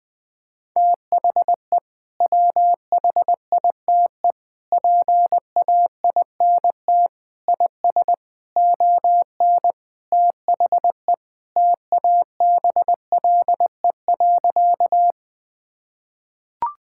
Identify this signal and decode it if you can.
Morse code, heard as a keyed tone, about 20 wpm, 700 Hz. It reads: THE WHITE PAINT IS ON THE TABLE.
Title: Sentences from Top 1000 Words - Review Part 6 - 20wpm